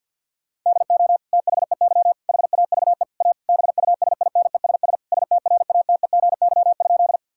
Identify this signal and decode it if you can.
Morse code, heard as a keyed tone, about 53 wpm, 700 Hz. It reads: DO THEY HAVE A BUSINESS STRATEGY?
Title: Sentences from Top 1000 Words - Review Part 5 - 50wpm